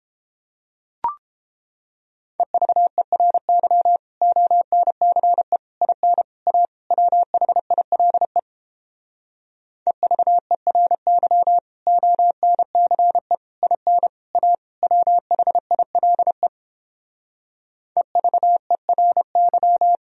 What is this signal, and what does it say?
Morse code, heard as a keyed tone, about 31 wpm, 700 Hz. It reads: EVERY ONCE IN A WHILE EVERY ONCE IN A WHILE EVERY